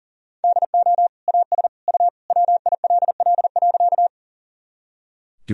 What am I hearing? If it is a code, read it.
Morse code, heard as a keyed tone, about 40 wpm, 700 Hz. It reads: DO AS U WILL.
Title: Sentences from Top 100 Words and Abbreviations 40wpm